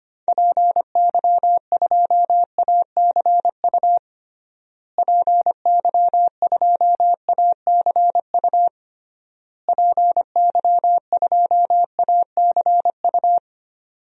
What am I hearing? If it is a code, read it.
Morse code, heard as a keyed tone, about 25 wpm, 700 Hz. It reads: PY2ACU PY2ACU PY2ACU
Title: DXCC-100 Call Signs - 6 Characters - Repeated 3x - 25wpm